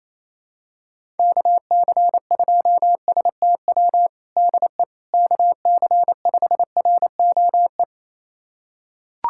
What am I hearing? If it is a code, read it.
Morse code, heard as a keyed tone, about 28 wpm, 700 Hz. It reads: KC2STW DE KC5ROE
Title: USA Callsign DE Callsign - 6 Characters - 28wpm